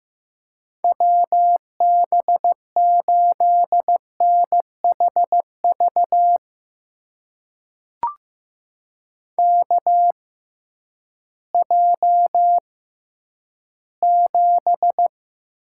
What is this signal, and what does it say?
Morse code, heard as a keyed tone, about 15 wpm, spaced as for 10 wpm, 700 Hz. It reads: WB8NHV KJ7
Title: USA Call Signs - 6 Characters - 6x Character Spacing - 15wpm